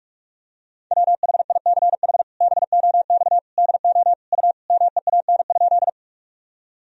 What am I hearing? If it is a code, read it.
Morse code, heard as a keyed tone, about 46 wpm, 700 Hz. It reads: WHICH BOX DO U MEAN?
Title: Sentences from Top 500 Words and Abbreviations 45wpm